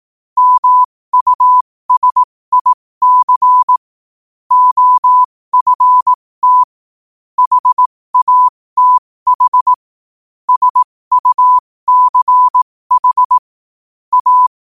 Morse code, heard as a keyed tone, about 18 wpm, spaced as for 15 wpm, 1000 Hz. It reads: MUSIC OFT HATH SUCH A
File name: qotdfiles / 15wpm